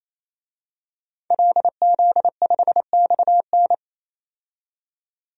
Morse code, heard as a keyed tone, about 28 wpm, 700 Hz. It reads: LZ5XN
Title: DXCC-100 Call Signs - 5 Characters - 28wpm